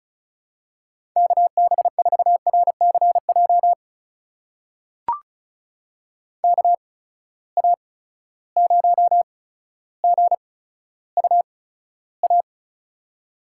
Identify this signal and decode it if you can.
Morse code, heard as a keyed tone, about 36 wpm, spaced as for 15 wpm, 700 Hz. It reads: KB4RCJ KA0GUA
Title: USA Call Signs - 6 Characters - 8x Character Spacing - 35wpm